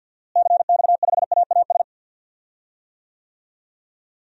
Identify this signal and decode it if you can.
Morse code, heard as a keyed tone, about 52 wpm, 700 Hz. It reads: CX5AAS
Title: DXCC-100 Call Signs - 6 Characters - Repeated 3x - 50wpm